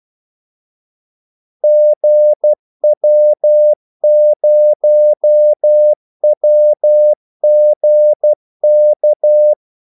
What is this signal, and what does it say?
Morse code, heard as a keyed tone, about 12 wpm, 600 Hz. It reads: GW0WGK
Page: 2 ekstra ordmellemrum mellem kaldesignalerne